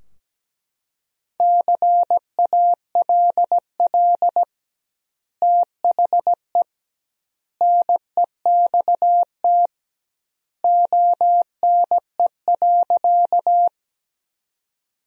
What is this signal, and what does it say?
Morse code, heard as a keyed tone, about 17 wpm, 700 Hz. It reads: CALL THE NEXT ONE.
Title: 2x Word-Spacing - Sentences from Top 300 Words - 17wpm